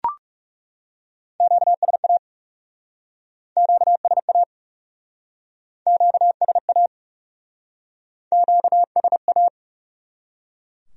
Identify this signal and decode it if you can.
Morse code, heard as a keyed tone, about 36 wpm, 700 Hz. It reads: QSA QSA QSA QSA
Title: Speed-Racing Q-Codes - 30wpm